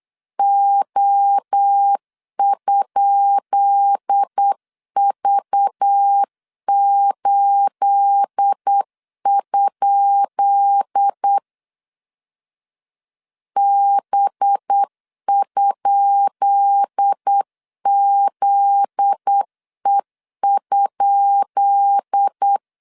Morse code, heard as a keyed tone, about 8 wpm, 800 Hz. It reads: O?V8? B?ZE?
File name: LEZIONI / 10 SIGNS / 10B INT 8wpm 3534